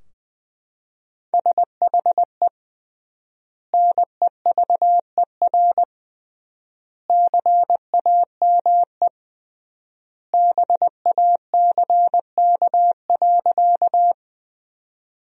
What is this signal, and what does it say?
Morse code, heard as a keyed tone, about 20 wpm, 700 Hz. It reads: SHE NEVER CAME BACK.
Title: 3x Word-Spacing - Sentences from Top 300 Words - 20wpm